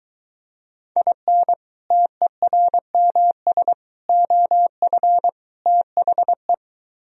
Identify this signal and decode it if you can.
Morse code, heard as a keyed tone, about 23 wpm, 700 Hz. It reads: IN TERMS OF THE